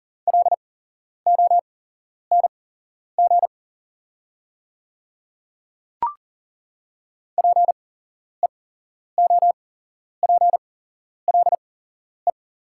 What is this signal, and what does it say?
Morse code, heard as a keyed tone, about 41 wpm, spaced as for 11 wpm, 700 Hz. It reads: LONG PEOPLE
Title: 8x Character Spacing - Top 100 Words - 40wpm